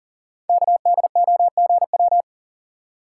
Morse code, heard as a keyed tone, about 40 wpm, 700 Hz. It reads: KDOGW
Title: Code Groups - 5 Letters - 40wpm